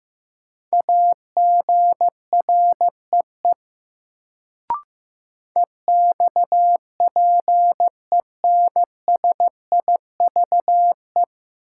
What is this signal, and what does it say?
Morse code, heard as a keyed tone, about 16 wpm, 700 Hz. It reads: AGREE EXPENSIVE
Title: Sentences from Top 1000 Words - Review Part 6 - New Word Drill - 15wpm